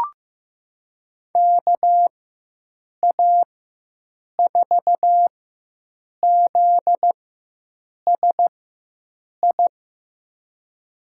Morse code, heard as a keyed tone, about 15 wpm, spaced as for 7 wpm, 700 Hz. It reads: KA4ZSI
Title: USA Call Signs - 4x Character Spacing - 15wpm